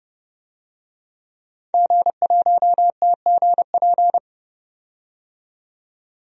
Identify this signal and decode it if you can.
Morse code, heard as a keyed tone, about 32 wpm, 700 Hz. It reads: G1TGP
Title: DXCC-100 Call Signs - 5 Characters - 30wpm